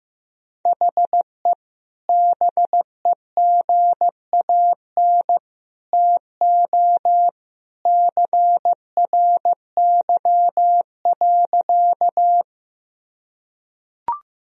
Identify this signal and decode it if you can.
Morse code, heard as a keyed tone, about 15 wpm, 700 Hz. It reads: HE BEGAN TO CRY.
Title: Sentences from Top 1000 Words - Review Part 2 - 15wpm